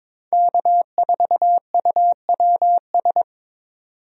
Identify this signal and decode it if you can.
Morse code, heard as a keyed tone, about 22 wpm, 700 Hz. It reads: K4UWS
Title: USA Call Signs - 5 Characters - 22wpm